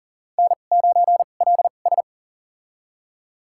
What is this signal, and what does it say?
Morse code, heard as a keyed tone, about 37 wpm, spaced as for 28 wpm, 700 Hz. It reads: N9LS
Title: USA Call Signs - 4 Characters - 2x Character Spacing - 40wpm